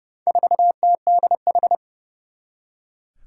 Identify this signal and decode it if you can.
Morse code, heard as a keyed tone, about 30 wpm, 700 Hz. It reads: 4TDH
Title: Four Challenging Letter-Number Combinations 30wpm